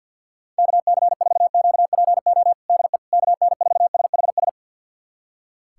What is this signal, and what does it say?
Morse code, heard as a keyed tone, about 46 wpm, 690 Hz. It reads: KC4QPO DE KN4SHS
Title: USA Callsign DE Callsign 50wpm